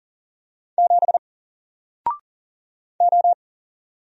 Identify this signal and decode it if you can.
Morse code, heard as a keyed tone, about 40 wpm, 700 Hz. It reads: 7 O
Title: Single Letter-Number Alt 40wpm